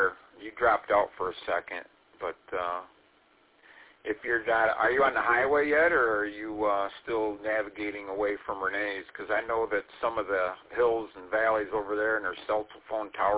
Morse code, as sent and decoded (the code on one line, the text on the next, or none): none